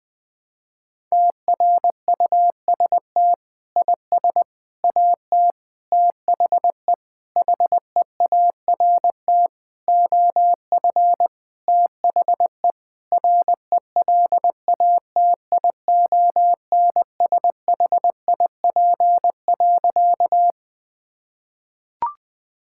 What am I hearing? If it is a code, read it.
Morse code, heard as a keyed tone, about 20 wpm, 700 Hz. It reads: TRUST IS AT THE HEART OF THE RELATIONSHIP.
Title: Sentences with 6-Word Phrases - 20wpm